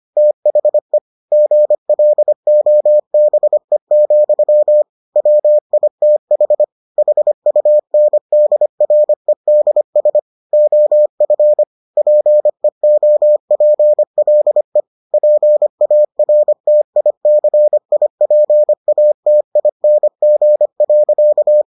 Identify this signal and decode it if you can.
Morse code, heard as a keyed tone, about 25 wpm, 600 Hz. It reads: THE GLOBE, WITH HUNDREDS OF PEOPLE PARTICIPATING.